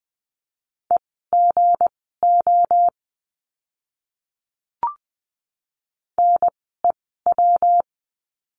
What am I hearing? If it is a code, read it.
Morse code, heard as a keyed tone, about 20 wpm, spaced as for 15 wpm, 700 Hz. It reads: EGO NEW
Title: 160 Lesson 26f - Letter g - Words 20wpm